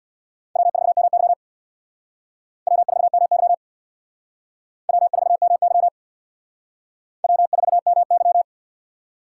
Morse code, heard as a keyed tone, about 61 wpm, 700 Hz. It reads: W4MY W4MY W4MY W4MY
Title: Speed Racing - USA 1x2 Call Signs 50wpm